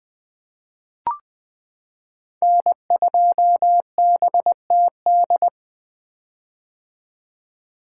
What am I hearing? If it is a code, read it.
Morse code, heard as a keyed tone, about 20 wpm, 700 Hz. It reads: N2BTD
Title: 2023 POTA Activator Call Signs - 20wpm